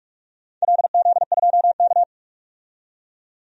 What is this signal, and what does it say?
Morse code, heard as a keyed tone, about 46 wpm, 700 Hz. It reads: LZ2K
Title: DXCC-100 Call Signs - 4 Characters - 45wpm